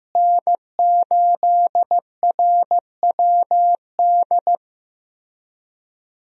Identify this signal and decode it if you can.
Morse code, heard as a keyed tone, about 16 wpm, 700 Hz. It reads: N8RWD
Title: USA Call Signs - 5 Characters - Repeated 2x - 15wpm